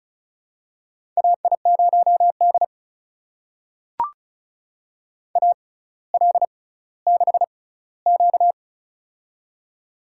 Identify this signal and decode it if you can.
Morse code, heard as a keyed tone, about 36 wpm, spaced as for 18 wpm, 700 Hz. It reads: AI0D AL6Q
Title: USA Call Signs - 4 Characters - 6x Character Spacing - 35wpm